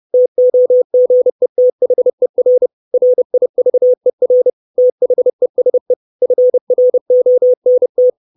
Morse code, heard as a keyed tone, about 30 wpm, 500 Hz. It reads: TOGETHER RIVER THESE FRONT
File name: 632 Words 30WPM